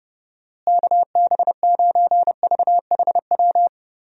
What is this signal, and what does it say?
Morse code, heard as a keyed tone, about 30 wpm, 700 Hz. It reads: KB9VHW
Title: USA Call Signs - 6 Characters - Repeated 2x - 30wpm